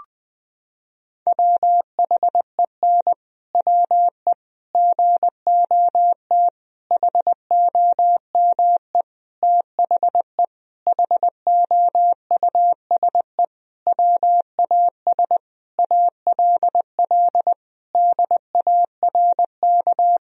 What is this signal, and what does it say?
Morse code, heard as a keyed tone, about 20 wpm, 700 Hz. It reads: WHEN WE GOT HOME THE HOUSE WAS ALL DARK
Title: Sentences from Top 500 Words and Abbreviations 20wpm